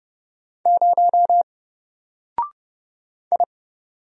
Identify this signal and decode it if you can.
Morse code, heard as a keyed tone, about 30 wpm, 700 Hz. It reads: 0 I
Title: Single Character (Letter:Number:Punctuation:Prosign) Alt 30wpm